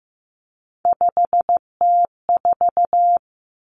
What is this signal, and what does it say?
Morse code, heard as a keyed tone, about 15 wpm, 700 Hz. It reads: 5T4
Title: Three Challenging Letter-Number Combinations 15wpm